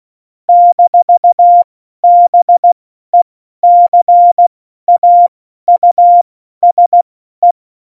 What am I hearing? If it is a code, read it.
Morse code, heard as a keyed tone, about 16 wpm, spaced as for 12 wpm, 700 Hz. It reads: - BECAUSE
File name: Morsekurs / Witze